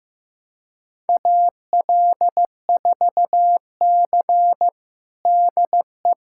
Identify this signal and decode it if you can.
Morse code, heard as a keyed tone, about 15 wpm, 700 Hz. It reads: AL4C DE
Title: USA Callsign DE Callsign - 4 Characters - 15wpm